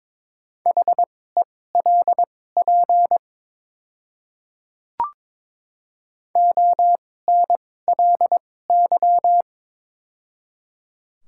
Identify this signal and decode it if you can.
Morse code, heard as a keyed tone, about 22 wpm, spaced as for 16 wpm, 700 Hz. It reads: HELP ONLY
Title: 2x Character Spacing - Top 200 Words - 22wpm